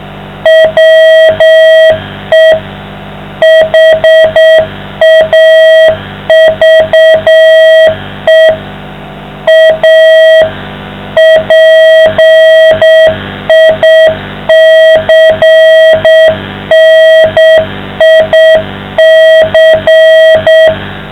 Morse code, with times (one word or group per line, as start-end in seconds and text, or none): 0.5-2.5 WE
3.4-8.5 HAVE
9.5-10.4 A
11.2-20.7 PICNIC